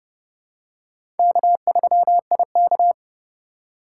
Morse code, heard as a keyed tone, about 30 wpm, 700 Hz. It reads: K3IK
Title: USA Call Signs - 4 Characters - 4x Character Spacing - 30wpm